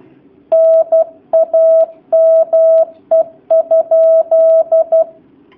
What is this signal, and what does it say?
Morse code, heard as a keyed tone, about 12 wpm, 650 Hz. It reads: NAME?
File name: jh8cbh / 2019cw / sp11- w